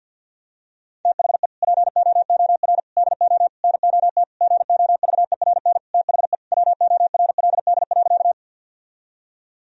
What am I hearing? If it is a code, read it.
Morse code, heard as a keyed tone, about 46 wpm, 700 Hz. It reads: THE POOR DO NOT GOVERN THE WORLD.
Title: Sentences from Top 1000 Words - Review Part 2 - 50wpm